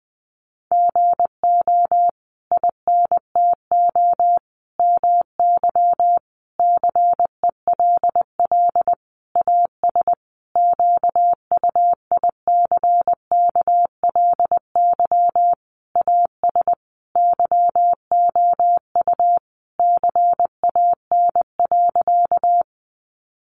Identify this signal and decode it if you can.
Morse code, heard as a keyed tone, about 20 wpm, 700 Hz. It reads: GO INTO MY CELL AS QUICKLY AS YOU CAN.
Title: Sentences from Top 1100 Words 20wpm